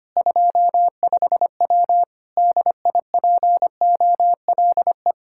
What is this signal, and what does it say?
Morse code, heard as a keyed tone, about 25 wpm, 700 Hz. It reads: 25W DIPOLE
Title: K1USN SST QSO Practice 25wpm